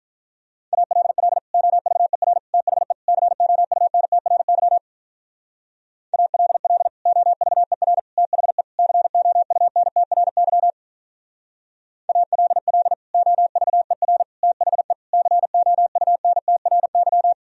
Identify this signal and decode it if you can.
Morse code, heard as a keyed tone, about 44 wpm, 700 Hz. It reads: ALL OVER THE COUNTRY ALL OVER THE COUNTRY ALL OVER THE COUNTRY